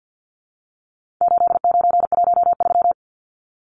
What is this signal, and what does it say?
Morse code, heard as a keyed tone, about 50 wpm, 700 Hz. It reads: 8913
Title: Numbers - 4-Digit 50wpm